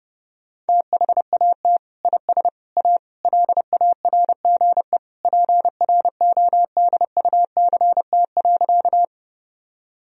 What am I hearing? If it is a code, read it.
Morse code, heard as a keyed tone, about 30 wpm, 700 Hz. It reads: THAT IS A LARGE PRODUCT.
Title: Sentences from Top 1000 Words - Review Part 2 - 30wpm